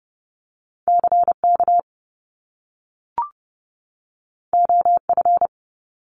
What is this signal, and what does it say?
Morse code, heard as a keyed tone, about 30 wpm, 700 Hz. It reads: CK OF
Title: Top Two Letter Combinations in Words 30wpm